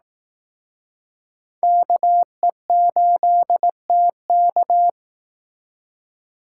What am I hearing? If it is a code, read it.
Morse code, heard as a keyed tone, about 18 wpm, 700 Hz. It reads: KE8TK
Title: Speed Racing - USA Call Signs 15wpm